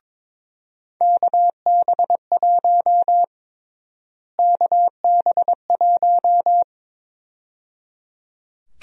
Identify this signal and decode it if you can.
Morse code, heard as a keyed tone, about 22 wpm, 700 Hz. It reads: KB1 KB1
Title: USA Call Sign Prefixes - Repeated 2x - 22wpm